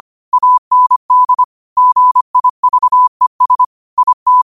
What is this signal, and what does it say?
Morse code, heard as a keyed tone, about 25 wpm, 1000 Hz. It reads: AND GIVES IT
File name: qotdfiles / 25wpm